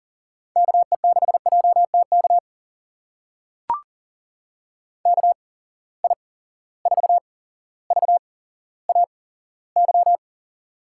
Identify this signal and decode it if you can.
Morse code, heard as a keyed tone, about 42 wpm, spaced as for 17 wpm, 700 Hz. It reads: KE6JTK KI4VAY
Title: USA Call Signs - 6 Characters - 8x Character Spacing - 40wpm